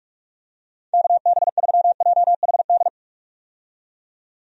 Morse code, heard as a keyed tone, about 46 wpm, 700 Hz. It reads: KB3JHD
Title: USA Call Signs - 6 Characters - 45wpm